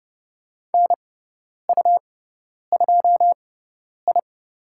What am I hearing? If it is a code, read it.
Morse code, heard as a keyed tone, about 30 wpm, spaced as for 10 wpm, 700 Hz. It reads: NU2I
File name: Lesson 18g - Letter m - Call Signs 30wpm